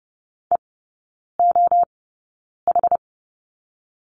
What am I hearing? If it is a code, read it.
Morse code, heard as a keyed tone, about 30 wpm, spaced as for 9 wpm, 700 Hz. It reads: EOH
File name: 081 Lesson 14e - Number 2 - Three Characters 30wpm